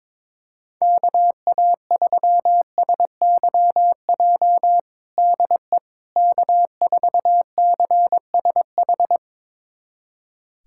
Morse code, heard as a keyed tone, about 22 wpm, 700 Hz. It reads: KA3SYJ DE K4CSH
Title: USA Callsign DE Callsign 22wpm